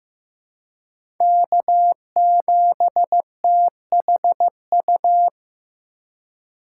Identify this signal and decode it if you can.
Morse code, heard as a keyed tone, about 16 wpm, 700 Hz. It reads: K7THU
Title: USA Call Signs - 5 Characters - 4x Character Spacing - 15wpm